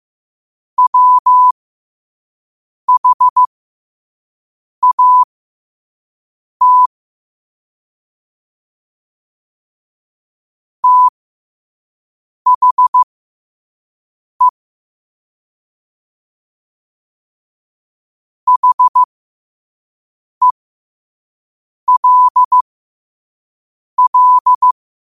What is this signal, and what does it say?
Morse code, heard as a keyed tone, about 15 wpm, spaced as for 5 wpm, 1000 Hz. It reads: WHAT THE HELL
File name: qotdfiles / 5wpm